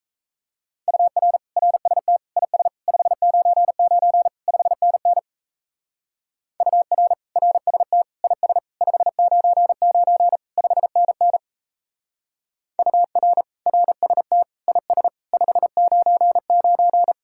Morse code, heard as a keyed tone, about 32 wpm, 700 Hz. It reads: UR RST IS 599 5NN UR RST IS 599 5NN UR RST IS 599
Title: Speed-Racing Common QSO Phrases - 28wpm